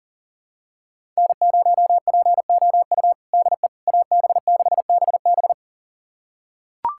Morse code, heard as a keyed tone, about 40 wpm, 700 Hz. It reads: N0POU DE AB6BB
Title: USA Callsign DE Callsign - 5 Characters - 40wpm